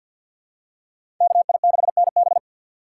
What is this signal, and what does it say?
Morse code, heard as a keyed tone, about 51 wpm, 700 Hz. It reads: KI6NB
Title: USA Call Signs - 3x Character Spacing - 50wpm